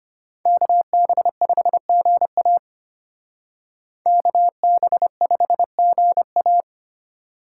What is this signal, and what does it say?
Morse code, heard as a keyed tone, about 32 wpm, 700 Hz. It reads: KB5GA KB5GA